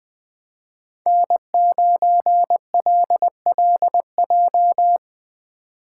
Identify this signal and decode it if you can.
Morse code, heard as a keyed tone, about 19 wpm, 700 Hz. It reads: N9LLJ
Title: Speed Racing - USA 1x3 Call Signs 17wpm